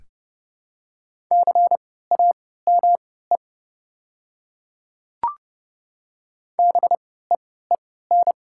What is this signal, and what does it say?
Morse code, heard as a keyed tone, about 30 wpm, spaced as for 17 wpm, 700 Hz. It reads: CAME BEEN
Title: 3x Character Spacing - Top 200 Words - 30wpm